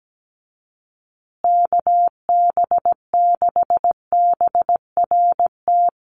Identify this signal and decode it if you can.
Morse code, heard as a keyed tone, about 17 wpm, 700 Hz. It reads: KB6BRT
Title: USA Call Signs Repeated 3x 17wpm